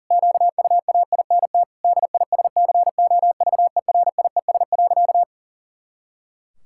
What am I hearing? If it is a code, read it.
Morse code, heard as a keyed tone, about 40 wpm, 700 Hz. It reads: QUAINT DISCOVERIES.
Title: Pangram Challenge v2 - 40wpm